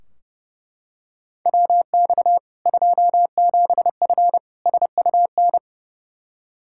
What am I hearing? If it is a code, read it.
Morse code, heard as a keyed tone, about 30 wpm, 700 Hz. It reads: WX 27F SUN
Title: Practice QSO - Exchange 2 - Info - 30wpm